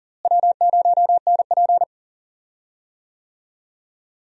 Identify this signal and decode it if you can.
Morse code, heard as a keyed tone, about 40 wpm, 700 Hz. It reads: W0NP
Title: USA Call Signs - 4 Characters - Repeated 2x - 40wpm